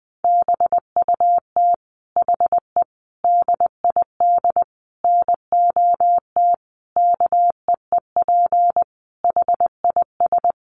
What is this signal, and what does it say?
Morse code, heard as a keyed tone, about 20 wpm, 700 Hz. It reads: BUT HE DID NOT KEEP HIS
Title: Sentences from Top 1500 Words 20wpm